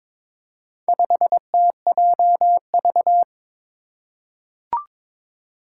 Morse code, heard as a keyed tone, about 22 wpm, 700 Hz. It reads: 5TJV
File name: practice / 22wpm / Four Letter-Number Combinations - 22wpm